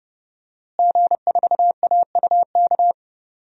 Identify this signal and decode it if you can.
Morse code, heard as a keyed tone, about 30 wpm, 700 Hz. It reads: G4AUK
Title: DXCC-100 Call Signs - 5 Characters - Repeated 3x - 30wpm